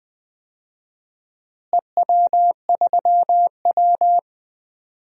Morse code, heard as a keyed tone, about 21 wpm, 700 Hz. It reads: EW3W
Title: DXCC-100 Call Signs - 4 Characters - Repeated 2x - 20wpm